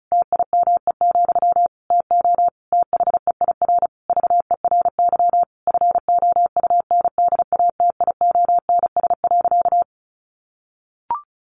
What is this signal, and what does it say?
Morse code, heard as a keyed tone, about 34 wpm, 700 Hz. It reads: TIME, TO THEIR VERY FOUNDATIONS.
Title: A Sound Of Thunder 35wpm